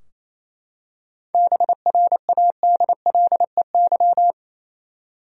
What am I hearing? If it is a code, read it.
Morse code, heard as a keyed tone, about 27 wpm, 700 Hz. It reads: BRADLEY
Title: USA Common Names - 28wpm